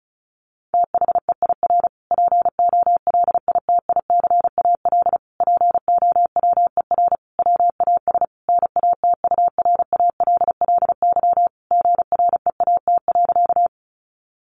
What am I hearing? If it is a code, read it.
Morse code, heard as a keyed tone, about 34 wpm, 700 Hz. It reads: THEIR POLITICAL POWER WAS NATURALLY GREAT.
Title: Sentences from Top 1300 Words 35wpm